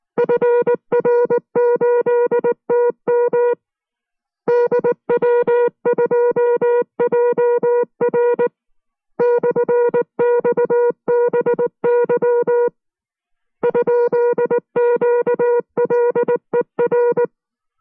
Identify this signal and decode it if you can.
Morse code, heard as a keyed tone, about 19 wpm, 480 Hz. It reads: FR8TM DW2JR /XBY ?QLER